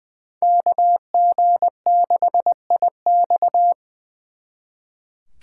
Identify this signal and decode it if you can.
Morse code, heard as a keyed tone, about 23 wpm, 700 Hz. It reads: KG6IX
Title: Speed Racing - USA 2x2 Call Signs 20wpm